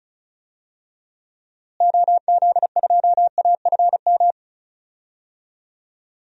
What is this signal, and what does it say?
Morse code, heard as a keyed tone, about 36 wpm, 700 Hz. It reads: OZ2AFM
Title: DXCC-100 Call Signs - 6 Characters - Repeated 2x - 35wpm